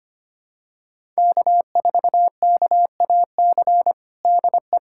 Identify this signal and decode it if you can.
Morse code, heard as a keyed tone, about 25 wpm, 700 Hz. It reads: K4KAC DE
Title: USA Callsign DE Callsign 25wpm